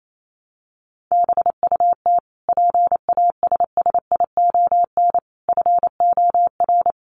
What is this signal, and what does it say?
Morse code, heard as a keyed tone, about 28 wpm, 700 Hz. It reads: BUT PASSION FOR
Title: Sentences from Top 1300 Words 28wpm